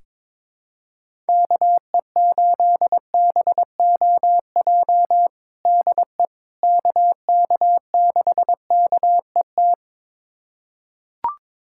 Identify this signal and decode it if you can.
Morse code, heard as a keyed tone, about 22 wpm, 700 Hz. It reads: KE8BOJ DE KK6KET
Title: USA Callsign DE Callsign - 6 Characters - 22wpm